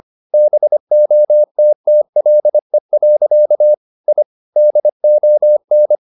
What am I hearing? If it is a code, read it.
Morse code, heard as a keyed tone, about 25 wpm, 600 Hz. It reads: BOTTLE. I DON
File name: the old man and the sea part 1 25wpm